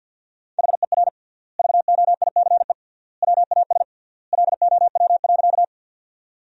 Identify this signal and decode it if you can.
Morse code, heard as a keyed tone, about 50 wpm, 700 Hz. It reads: HER VOICE WAS LOW.
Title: 3x Word-Spacing - Sentences from Top 500 Words - 50wpm